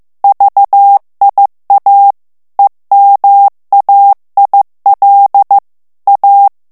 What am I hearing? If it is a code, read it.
Morse code, heard as a keyed tone, about 15 wpm, 800 Hz. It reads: VIA EMAIL A
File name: ITALIEN texte clair 1 vitesse 14 mots minute